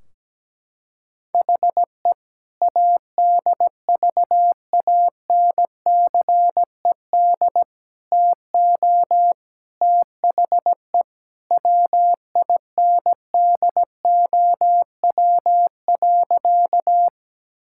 Morse code, heard as a keyed tone, about 17 wpm, 700 Hz. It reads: HE ADVANCED TO THE WINDOW.
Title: Sentences from Top 1600 Words 17wpm